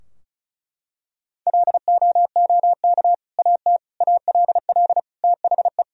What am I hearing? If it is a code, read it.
Morse code, heard as a keyed tone, about 34 wpm, 700 Hz. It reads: LOOK AT ALL THE
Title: Sentences from Top 1000 Words - Review Part 4 - 35wpm